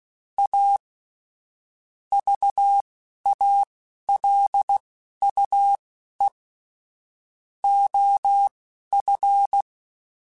Morse code, heard as a keyed tone, about 16 wpm, spaced as for 11 wpm, 800 Hz. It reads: A VALUE OF
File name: Technical Subjects / Technical Subjects-mp3s / XC @10